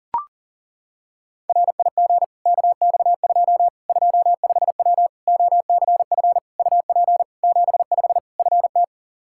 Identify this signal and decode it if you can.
Morse code, heard as a keyed tone, about 40 wpm, 700 Hz. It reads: RIG KX2 25W OCF UP 75 FT
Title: K1USN SST QSO Practice 40wpm